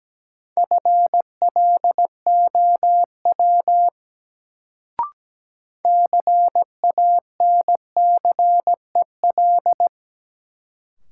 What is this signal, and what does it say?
Morse code, heard as a keyed tone, about 17 wpm, 700 Hz. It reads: FLOW CANCEL
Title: Sentences from Top 1200 Words - New Word Drill Alt - 17wpm